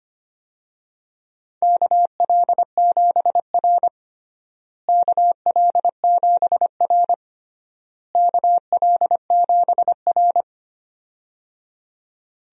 Morse code, heard as a keyed tone, about 25 wpm, 700 Hz. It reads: KL7R KL7R KL7R
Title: DXCC-100 Call Signs - 4 Characters - Repeated 3x - 25wpm